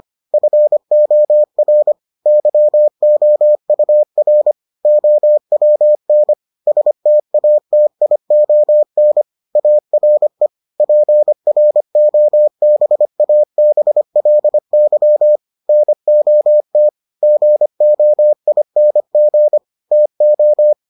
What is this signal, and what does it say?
Morse code, heard as a keyed tone, about 25 wpm, 600 Hz. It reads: FOR YOUR OWN STATION ARE PROBABLY NOT GOING TO